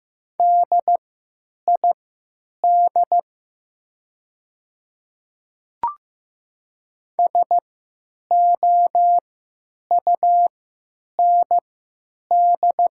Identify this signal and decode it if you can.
Morse code, heard as a keyed tone, about 15 wpm, spaced as for 8 wpm, 700 Hz. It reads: DID SOUND
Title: 3x Character Spacing - Top 100 Words - 15wpm